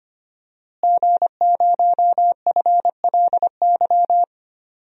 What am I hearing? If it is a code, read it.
Morse code, heard as a keyed tone, about 25 wpm, 700 Hz. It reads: G0FLY
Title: DXCC-100 Call Signs - 5 Characters - Repeated 3x - 25wpm